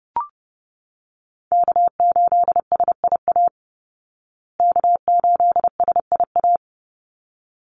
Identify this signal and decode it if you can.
Morse code, heard as a keyed tone, about 30 wpm, 700 Hz. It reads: K8SIA K8SIA
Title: Call Signs Repeated 3x - 30wpm